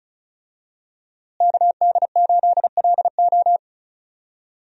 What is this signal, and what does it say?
Morse code, heard as a keyed tone, about 34 wpm, 700 Hz. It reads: KD8LO
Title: USA Call Signs - 5 Characters - Repeated 3x - 35wpm